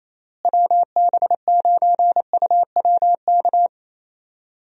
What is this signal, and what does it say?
Morse code, heard as a keyed tone, about 28 wpm, 700 Hz. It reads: WB9UWK
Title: USA Call Signs - 6 Characters - Repeated 2x - 28wpm